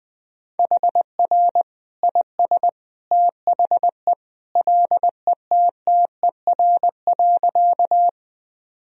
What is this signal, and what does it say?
Morse code, heard as a keyed tone, about 20 wpm, 700 Hz. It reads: HR IS THE LETTER.
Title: Sentences from Top 300 Words and Abbreviations 20wpm